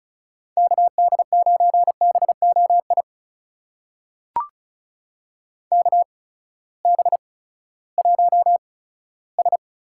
Morse code, heard as a keyed tone, about 36 wpm, spaced as for 18 wpm, 700 Hz. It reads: KD9BOI KB1S
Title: USA Call Signs - 6 Characters - 8x Character Spacing - 35wpm